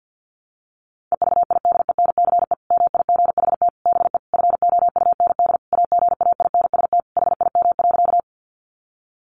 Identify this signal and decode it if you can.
Morse code, heard as a keyed tone, about 50 wpm, 700 Hz. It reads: EVIDENCE MIGHT BE FOUND AGAINST HIM.
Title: Sentences from Top 1300 Words 50wpm